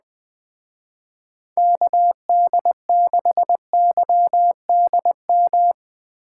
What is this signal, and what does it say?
Morse code, heard as a keyed tone, about 19 wpm, 700 Hz. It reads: KD6YDM
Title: Speed Racing - USA Call Signs 17wpm